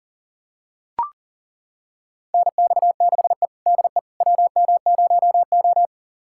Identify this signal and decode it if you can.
Morse code, heard as a keyed tone, about 40 wpm, 700 Hz. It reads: NX6E DE WM0O
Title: USA Callsign DE Callsign - 4 Characters - 40wpm